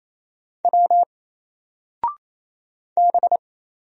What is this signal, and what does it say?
Morse code, heard as a keyed tone, about 28 wpm, 700 Hz. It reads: W B
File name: Single-Letter-Number-28wpm